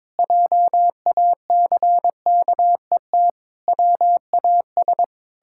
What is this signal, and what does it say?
Morse code, heard as a keyed tone, about 22 wpm, 700 Hz. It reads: JACKET WAS
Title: Pangram Challenge v2 - 22wpm